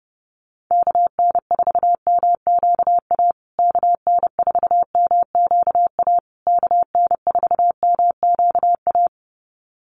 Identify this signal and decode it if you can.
Morse code, heard as a keyed tone, about 30 wpm, 700 Hz. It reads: KN4MQA KN4MQA KN4MQA
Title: USA Call Signs Repeated 3x 30wpm